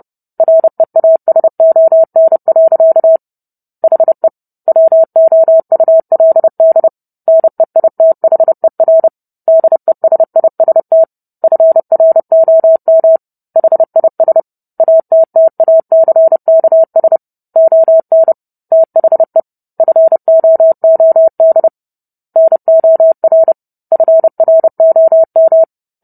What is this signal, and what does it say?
Morse code, heard as a keyed tone, about 30 wpm, 650 Hz. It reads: REASON. HE WOULD NEITHER DESIST FROM HIS ATTACKS ON THE FOOD NOR FROM